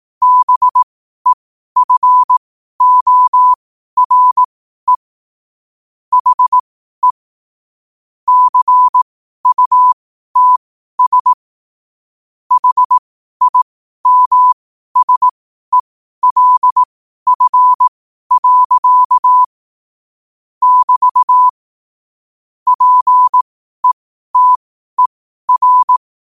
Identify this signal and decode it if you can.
Morse code, heard as a keyed tone, about 18 wpm, spaced as for 12 wpm, 1000 Hz. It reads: BEFORE HE CUTS HIMSELF. <BT> PETER
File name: qotdfiles / 12wpm